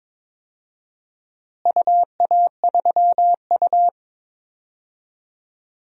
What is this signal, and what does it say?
Morse code, heard as a keyed tone, about 22 wpm, 700 Hz. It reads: UA3U
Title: DXCC-100 Call Signs - 4 Characters - Repeated 2x - 22wpm